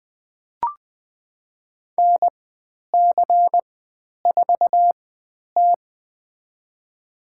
Morse code, heard as a keyed tone, about 20 wpm, spaced as for 10 wpm, 700 Hz. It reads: NC4T
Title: 095 Lesson 16g - Letter c - Call Signs 20wpm